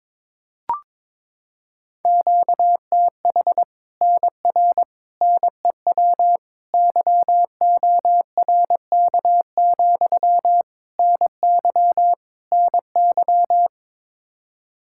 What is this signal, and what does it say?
Morse code, heard as a keyed tone, about 22 wpm, 700 Hz. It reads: QTH NR NEW YORK, NY NY
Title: Practice QSO - Exchange 1 - Beginning - 22wpm